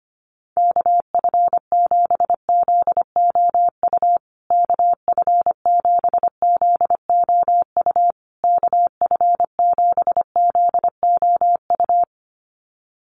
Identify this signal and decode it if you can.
Morse code, heard as a keyed tone, about 25 wpm, 700 Hz. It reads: KF7ZOU KF7ZOU KF7ZOU
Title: USA Call Signs Repeated 3x 25wpm